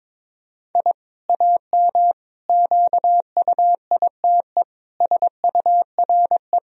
Morse code, heard as a keyed tone, about 22 wpm, 700 Hz. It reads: I AM QUITE SURE
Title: Sentences from Top 1000 Words - Review Part 2 - 22wpm